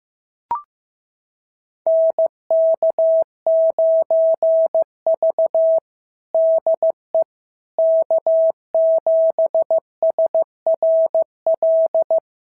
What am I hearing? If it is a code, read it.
Morse code, heard as a keyed tone, about 15 wpm, 650 Hz. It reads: NK9V DE K7SRL
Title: SOTA QSO Practice 15wpm